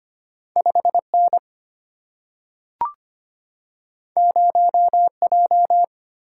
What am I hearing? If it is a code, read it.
Morse code, heard as a keyed tone, about 25 wpm, 700 Hz. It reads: 5N 0J
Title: Single-Letter Single-Number 25wpm